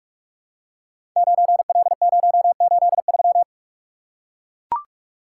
Morse code, heard as a keyed tone, about 44 wpm, 700 Hz. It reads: 9L083
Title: Code Groups - 5 Letter:Numbers - 45wpm